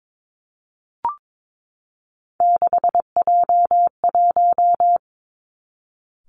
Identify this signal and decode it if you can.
Morse code, heard as a keyed tone, about 22 wpm, 700 Hz. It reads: 6J1
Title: Three Challenging Letter-Number Combinations 22wpm